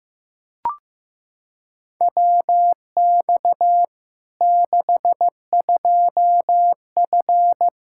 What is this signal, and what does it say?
Morse code, heard as a keyed tone, about 15 wpm, 700 Hz. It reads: WX 62F
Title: K1USN SST QSO Practice 15wpm